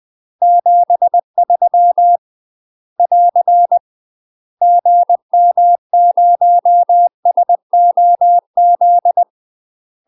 Morse code, heard as a keyed tone, about 20 wpm, 700 Hz. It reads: 73 <AR> GM0SOZ
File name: UK MASTER QSOs TRAINING / 20WPM / UK MASTER QSOs 0005